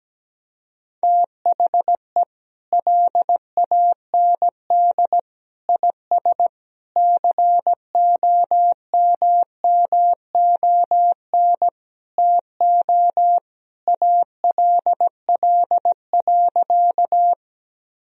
Morse code, heard as a keyed tone, about 17 wpm, 700 Hz. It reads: THE LAND IS COMMON TO ALL.
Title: Sentences from Top 1000 Words - Review Part 3 - 17wpm